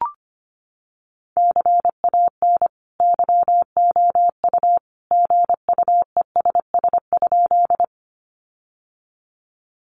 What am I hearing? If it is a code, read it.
Morse code, heard as a keyed tone, about 25 wpm, 700 Hz. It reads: CAN YOU GUESS?
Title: Sentences from Top 1500 Words 25wpm